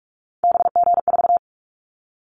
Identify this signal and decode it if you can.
Morse code, heard as a keyed tone, about 46 wpm, 700 Hz. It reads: BG4
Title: Callsign Prefixes for Top 100 DXCC - 45wpm